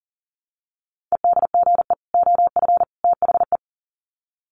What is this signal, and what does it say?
Morse code, heard as a keyed tone, about 40 wpm, 700 Hz. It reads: EDGE OF THE
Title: Sets of 3 Words 40wpm